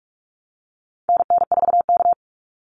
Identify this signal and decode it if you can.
Morse code, heard as a keyed tone, about 44 wpm, 700 Hz. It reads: NN4K
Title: Call Signs - 45wpm